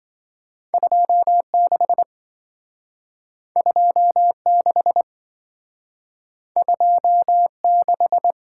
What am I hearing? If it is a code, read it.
Morse code, heard as a keyed tone, about 27 wpm, 700 Hz. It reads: 26 26 26